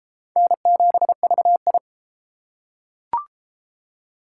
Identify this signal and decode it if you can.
Morse code, heard as a keyed tone, about 29 wpm, 700 Hz. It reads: N7VI